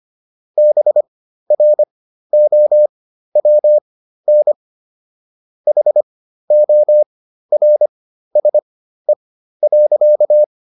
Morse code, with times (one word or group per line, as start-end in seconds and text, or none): 0.6-4.5 BROWN
5.7-10.4 HORSE.